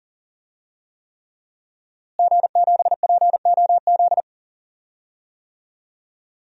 Morse code, heard as a keyed tone, about 40 wpm, 700 Hz. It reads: G7POZ
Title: DXCC-100 Call Signs - 5 Characters - Repeated 3x - 40wpm